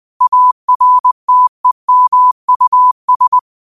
Morse code, heard as a keyed tone, about 20 wpm, 1000 Hz. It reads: ARTEMUS
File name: qotdfiles / 20wpm